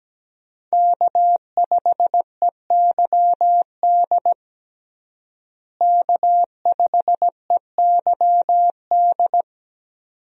Morse code, heard as a keyed tone, about 17 wpm, 700 Hz. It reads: K5EYD K5EYD
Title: USA Call Signs - 5 Characters - Repeated 2x - 17wpm